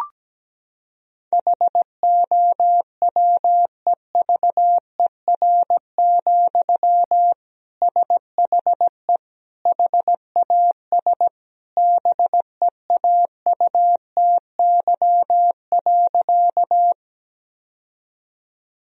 Morse code, heard as a keyed tone, about 17 wpm, 700 Hz. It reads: HOWEVER, SHE HAS BEAUTY.
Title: Sentences from Top 1000 Words - Review Part 2 - 17wpm